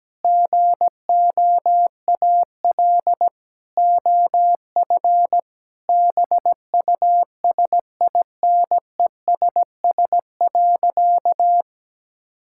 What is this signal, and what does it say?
Morse code, heard as a keyed tone, about 17 wpm, 700 Hz. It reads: GOAL OF BUSINESS.
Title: Sentences from Top 1000 Words - Review Part 5 - 17wpm